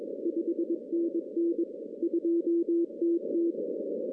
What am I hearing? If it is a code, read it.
Morse code, heard as a keyed tone, about 22 wpm, 340 Hz. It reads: 5NN 2TT
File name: 2016 arrl dx cw / 2016-ARRL-DX-CW-K5TR- - 0-25